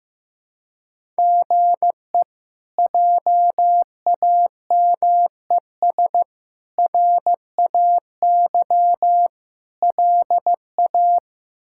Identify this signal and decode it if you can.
Morse code, heard as a keyed tone, about 15 wpm, 700 Hz. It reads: GE JAMES RAY LA
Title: K1USN SST QSO Practice - 15wpm